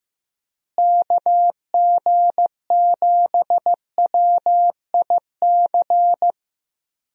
Morse code, heard as a keyed tone, about 15 wpm, 700 Hz. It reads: KG7WIC
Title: USA Call Signs - 6 Characters - Repeated 2x - 15wpm